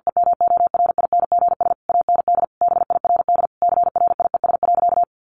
Morse code, heard as a keyed tone, about 50 wpm, 700 Hz. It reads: EMORINGS AND BIRD CRIES.
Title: A Sound Of Thunder 50wpm